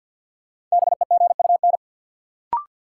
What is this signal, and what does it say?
Morse code, heard as a keyed tone, about 52 wpm, 700 Hz. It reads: BEGUN
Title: Sentences from Top 1000 Words - Review Part 6 - New Word Drill - 50wpm